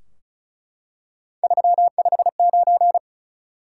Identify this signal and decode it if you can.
Morse code, heard as a keyed tone, about 36 wpm, 700 Hz. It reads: 359
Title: Top 180 QSO Elements 35wpm